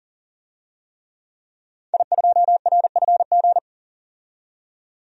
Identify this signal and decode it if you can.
Morse code, heard as a keyed tone, about 40 wpm, 700 Hz. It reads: I2RFG
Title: DXCC-100 Call Signs - 5 Characters - 40wpm